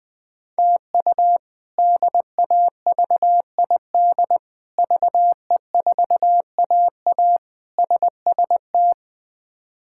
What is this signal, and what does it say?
Morse code, heard as a keyed tone, about 20 wpm, 700 Hz. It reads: TU DAVID VE4AA SST
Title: K1USN SST QSO Practice - 20wpm